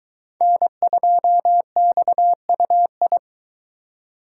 Morse code, heard as a keyed tone, about 22 wpm, 700 Hz. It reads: N2XUI